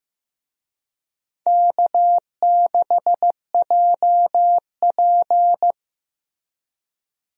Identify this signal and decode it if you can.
Morse code, heard as a keyed tone, about 15 wpm, 700 Hz. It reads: K6JP
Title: USA Call Signs - 4 Characters - Repeated 3x - 15wpm